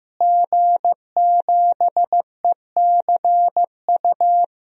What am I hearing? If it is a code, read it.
Morse code, heard as a keyed tone, about 15 wpm, 700 Hz. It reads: G7ECU
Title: DXCC-100 Call Signs - 5 Characters - Repeated 3x - 15wpm